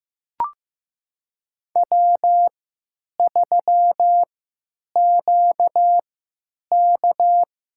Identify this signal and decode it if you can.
Morse code, heard as a keyed tone, about 15 wpm, spaced as for 9 wpm, 700 Hz. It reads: W3QK
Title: USA Call Signs - 3x Character Spacing - 15wpm